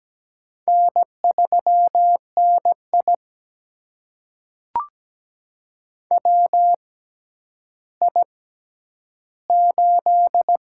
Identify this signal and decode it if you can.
Morse code, heard as a keyed tone, about 17 wpm, spaced as for 10 wpm, 700 Hz. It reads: N3NI WI8
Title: USA Call Signs - 4 Characters - 6x Character Spacing - 17wpm